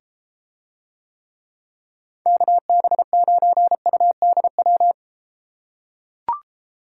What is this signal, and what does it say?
Morse code, heard as a keyed tone, about 33 wpm, 700 Hz. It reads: KB9UDW